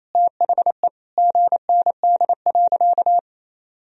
Morse code, heard as a keyed tone, about 28 wpm, 700 Hz. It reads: THE GND.
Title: Sentences from Top 500 Words and Abbreviations 28wpm